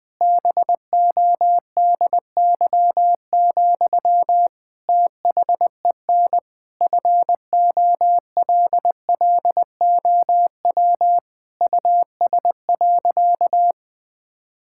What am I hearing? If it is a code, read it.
Morse code, heard as a keyed tone, about 20 wpm, 700 Hz. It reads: BODY, THEN FOLLOW US.
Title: Sentences from Top 1000 Words - Review Part 6 - 20wpm